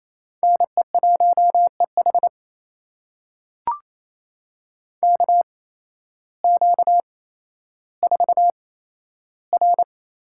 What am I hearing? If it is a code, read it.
Morse code, heard as a keyed tone, about 28 wpm, spaced as for 13 wpm, 700 Hz. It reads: NE1EH KQ4R
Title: USA Call Signs - 5 Characters - 8x Character Spacing - 28wpm